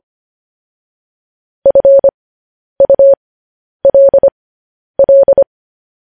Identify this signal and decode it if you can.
Morse code, heard as a keyed tone, about 25 wpm, spaced as for 10 wpm, 550 Hz. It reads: FULL